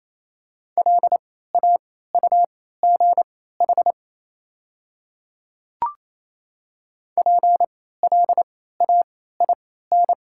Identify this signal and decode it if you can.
Morse code, heard as a keyed tone, about 28 wpm, spaced as for 16 wpm, 700 Hz. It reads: LAUGH PLAIN
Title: 3x Character Spacing - Top 500 Words - 28wpm